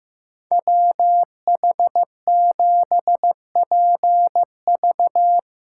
Morse code, heard as a keyed tone, about 17 wpm, 700 Hz. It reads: WH7PV
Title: Speed Racing - USA 2x2 Call Signs 15wpm